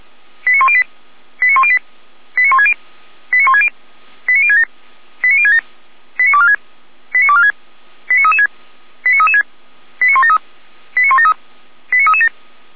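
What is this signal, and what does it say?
Five-tone selective call (ZVEI), heard as quick runs of five tones, tones of about 70 ms. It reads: 89108 89108 89160 89160 89067 89067 89246 89246 89336 89336 89173 89173 89337
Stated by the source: ZVEI five-tone selective call received by SDR, from feuerwehr-orlamuende.org (fire brigade Orlamünde) audio server.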